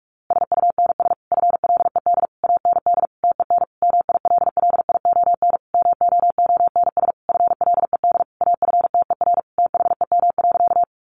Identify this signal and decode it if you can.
Morse code, heard as a keyed tone, about 44 wpm, 700 Hz. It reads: SUNS FLED AND TEN MILLION MOONS FLED AFTER THEM.
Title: A Sound Of Thunder 45wpm